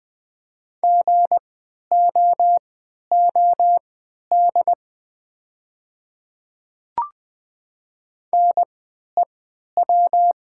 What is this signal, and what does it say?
Morse code, heard as a keyed tone, about 20 wpm, spaced as for 11 wpm, 700 Hz. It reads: GOOD NEW
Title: 3x Character Spacing - Top 200 Words - 20wpm